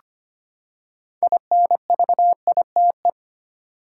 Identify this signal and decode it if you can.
Morse code, heard as a keyed tone, about 25 wpm, 700 Hz. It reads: INVITE
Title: Six Letter Words - 25wpm